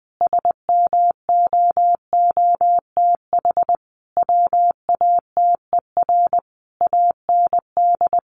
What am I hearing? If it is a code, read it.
Morse code, heard as a keyed tone, about 20 wpm, 700 Hz. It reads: SMOOTH WATER AND
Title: Sentences from Top 1400 Words 20wpm